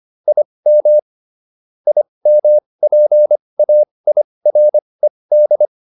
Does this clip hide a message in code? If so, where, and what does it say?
Yes: Morse code from 0.3 to 5.7 s, heard as a keyed tone, about 25 wpm, spaced as for 20 wpm, 600 Hz. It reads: IM IMPAIRED